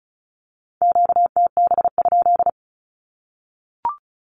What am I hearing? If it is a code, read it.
Morse code, heard as a keyed tone, about 34 wpm, 700 Hz. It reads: QTB?
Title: Q-Codes Repeated 3x 35wpm